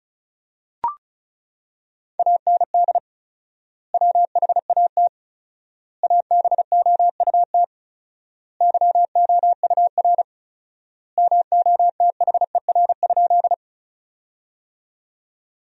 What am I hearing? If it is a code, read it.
Morse code, heard as a keyed tone, about 36 wpm, 700 Hz. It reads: AND WHAT ABOUT YOUR MOTHER?
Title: 4x Word-Spacing - Sentences from Top 200 Words - 35wpm